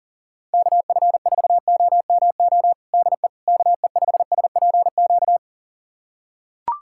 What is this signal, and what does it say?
Morse code, heard as a keyed tone, about 37 wpm, 700 Hz. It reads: KF4OMO DE KE5SPQ
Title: USA Callsign DE Callsign - 6 Characters - 40wpm